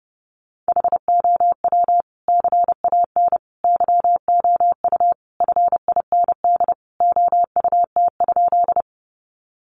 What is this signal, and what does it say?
Morse code, heard as a keyed tone, about 30 wpm, 700 Hz. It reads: HOW CAN YOU FIND OUT?
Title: Sentences from Top 200 Words 30wpm